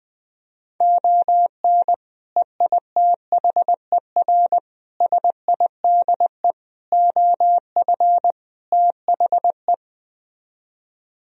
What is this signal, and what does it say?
Morse code, heard as a keyed tone, about 20 wpm, 700 Hz. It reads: ON EITHER SIDE OF THE
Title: Speed-Racing - Sets of 5 Words 17wpm